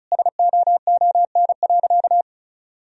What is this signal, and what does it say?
Morse code, heard as a keyed tone, about 36 wpm, 700 Hz. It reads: SOON.
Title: Sentences from Top 1000 Words - Review Part 1 - 35wpm